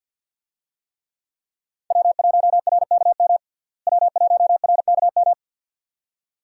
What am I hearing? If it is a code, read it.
Morse code, heard as a keyed tone, about 49 wpm, 700 Hz. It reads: W1RKM W1RKM
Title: USA Call Signs - 5 Characters - Repeated 2x - 50wpm